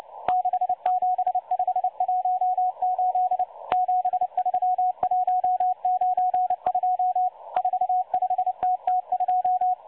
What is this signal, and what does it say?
Morse code, heard as a keyed tone, about 29 wpm, 730 Hz. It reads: 67518 73192 45TT2
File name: M12-Morse-Numbers-Station-8116-kHz-CW